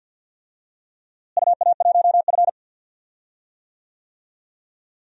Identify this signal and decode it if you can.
Morse code, heard as a keyed tone, about 50 wpm, 700 Hz. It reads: UA1F
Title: DXCC-100 Call Signs - 4 Characters - Repeated 2x - 50wpm